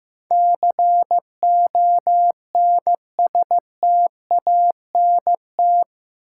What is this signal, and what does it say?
Morse code, heard as a keyed tone, about 16 wpm, 700 Hz. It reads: CONSTANT
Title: Sentences from Top 1500 Words - New Word Drill - 15wpm